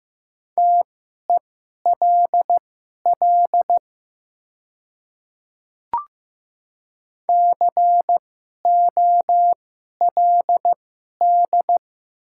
Morse code, heard as a keyed tone, about 15 wpm, spaced as for 11 wpm, 700 Hz. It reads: TELL COLD
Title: 2x Character Spacing - Top 500 Words - 15wpm